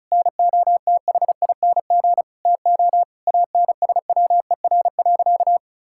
Morse code, heard as a keyed tone, about 34 wpm, 700 Hz. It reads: NOTHING TO ANSWER.
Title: Sentences from Top 1600 Words 35wpm